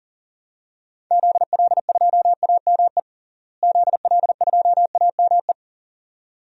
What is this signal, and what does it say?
Morse code, heard as a keyed tone, about 40 wpm, 700 Hz. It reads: ZL2AME ZL2AME
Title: DXCC-100 Call Signs - 6 Characters - Repeated 2x - 40wpm